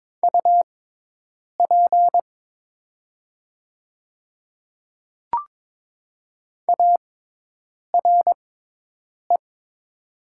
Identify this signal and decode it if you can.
Morse code, heard as a keyed tone, about 22 wpm, spaced as for 8 wpm, 700 Hz. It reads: UP ARE
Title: 6x Character Spacing - Top 300 Words - 22wpm